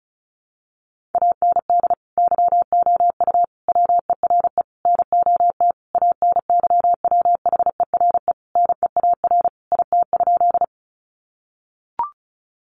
Morse code, heard as a keyed tone, about 34 wpm, 700 Hz. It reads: AND YOU WERE NOT ANYWHERE NEAR IT?
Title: Sentences from Top 1400 Words 35wpm